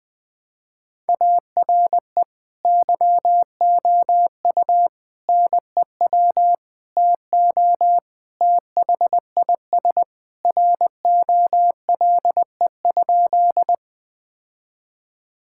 Morse code, heard as a keyed tone, about 21 wpm, 700 Hz. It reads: ARE YOU NEW TO THIS ROLE?
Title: Sentences from Top 1000 Words - Review Part 6 - 20wpm